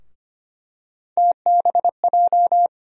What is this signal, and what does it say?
Morse code, heard as a keyed tone, about 25 wpm, 700 Hz. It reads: TBJ
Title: USA Call Sign Postfixes - 3 Letters 25wpm